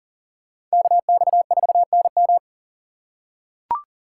Morse code, heard as a keyed tone, about 40 wpm, 700 Hz. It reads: KX4NM
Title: K1USN SST QSO Practice - 40wpm